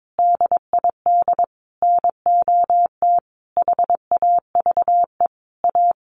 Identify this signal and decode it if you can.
Morse code, heard as a keyed tone, about 22 wpm, 700 Hz. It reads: DID NOT HAVE A
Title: Sets of 5 Words 22wpm